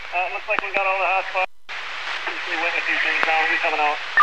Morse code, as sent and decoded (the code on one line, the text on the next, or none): none